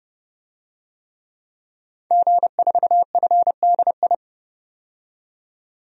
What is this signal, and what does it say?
Morse code, heard as a keyed tone, about 32 wpm, 700 Hz. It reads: G4FDI
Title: DXCC-100 Call Signs - 5 Characters - 30wpm